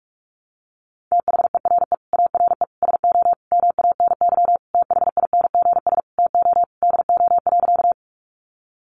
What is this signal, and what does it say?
Morse code, heard as a keyed tone, about 44 wpm, 700 Hz. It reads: THERE ARE SO MANY THINGS TO DO.
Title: Additional Easy Phrases 45wpm